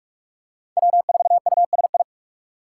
Morse code, heard as a keyed tone, about 44 wpm, 700 Hz. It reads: W4USI
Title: K1USN SST QSO Practice - 45wpm